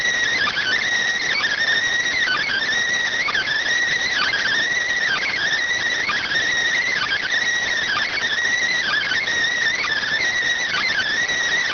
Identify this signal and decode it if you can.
Emergency Alert System (SAME) tones (header bursts), heard as harsh buzzing bursts. It is heard from 0.3 to 11.1 s.